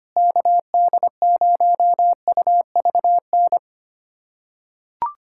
Morse code, heard as a keyed tone, about 25 wpm, 700 Hz. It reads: KD0UVN
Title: USA Callsign DE Callsign 25wpm